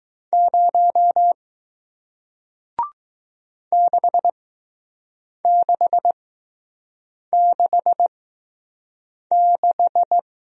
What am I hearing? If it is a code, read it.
Morse code, heard as a keyed tone, about 21 wpm, spaced as for 6 wpm, 700 Hz. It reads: 0 6666